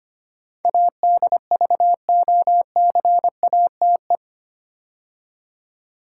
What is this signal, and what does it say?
Morse code, heard as a keyed tone, about 25 wpm, 700 Hz. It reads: ADVOCATE
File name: 8 Letter Words - First 3 Letters Given - 25wpm